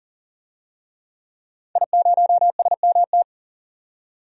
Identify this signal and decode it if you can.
Morse code, heard as a keyed tone, about 38 wpm, 700 Hz. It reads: I0SMT
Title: DXCC-100 Call Signs - 5 Characters - 40wpm